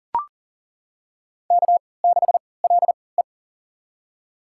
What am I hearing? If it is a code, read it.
Morse code, heard as a keyed tone, about 40 wpm, spaced as for 23 wpm, 700 Hz. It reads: K6LE
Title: USA Call Signs - 4 Characters - 3x Character Spacing - 40wpm